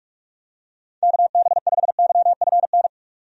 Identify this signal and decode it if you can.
Morse code, heard as a keyed tone, about 46 wpm, 700 Hz. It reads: KB5YFN
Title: USA Call Signs - 6 Characters - 45wpm